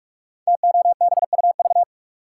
Morse code, heard as a keyed tone, about 44 wpm, 700 Hz. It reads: TOBUV
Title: Code Groups - 5 Letters - 45wpm